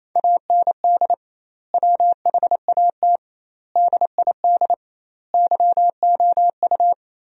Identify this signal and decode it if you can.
Morse code, heard as a keyed tone, about 28 wpm, 700 Hz. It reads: AND WHAT DID YOU
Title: 2x Word-Spacing - Sentences from Top 300 Words - 28wpm